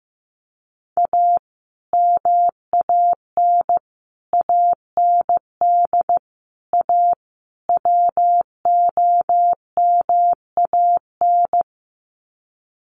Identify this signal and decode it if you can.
Morse code, heard as a keyed tone, about 15 wpm, 700 Hz. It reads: A MAN AND A WOMAN
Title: Sets of 5 Words 15wpm